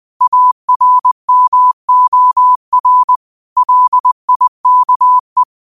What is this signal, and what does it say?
Morse code, heard as a keyed tone, about 20 wpm, 1000 Hz. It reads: ARMOR LIKE
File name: qotdfiles / 20wpm